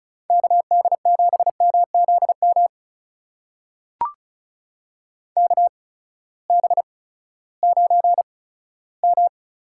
Morse code, heard as a keyed tone, about 36 wpm, spaced as for 18 wpm, 700 Hz. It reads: KD7MZM KB9M
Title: USA Call Signs - 6 Characters - 8x Character Spacing - 35wpm